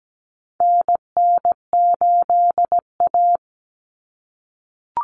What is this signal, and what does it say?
Morse code, heard as a keyed tone, about 17 wpm, 700 Hz. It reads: NN8A
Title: Common Cut Numbers - 4-Digit 17wpm